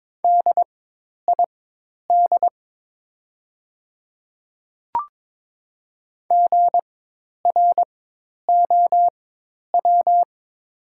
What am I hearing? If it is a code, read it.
Morse code, heard as a keyed tone, about 22 wpm, spaced as for 10 wpm, 700 Hz. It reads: DID GROW
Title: 4x Character Spacing - Top 300 Words - 22wpm